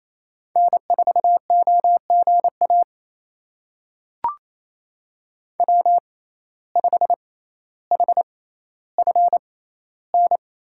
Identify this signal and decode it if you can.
Morse code, heard as a keyed tone, about 28 wpm, spaced as for 14 wpm, 700 Hz. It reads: N4OGA W5HFN
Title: USA Call Signs - 5 Characters - 6x Character Spacing - 28wpm